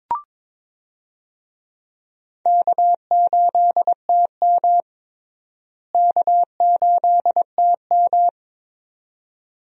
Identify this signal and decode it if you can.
Morse code, heard as a keyed tone, about 21 wpm, 700 Hz. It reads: K8TM K8TM
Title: USA Call Signs - 4 Characters - Repeated 2x - 22wpm